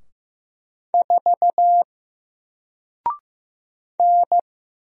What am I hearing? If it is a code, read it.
Morse code, heard as a keyed tone, about 15 wpm, 700 Hz. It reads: 4 N
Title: Single Letter-Number Alt 15wpm